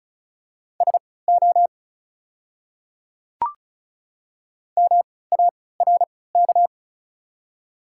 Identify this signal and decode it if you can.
Morse code, heard as a keyed tone, about 36 wpm, spaced as for 20 wpm, 690 Hz. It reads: SO MARK
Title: 3x Character Spacing - Top 300 Words - 35wpm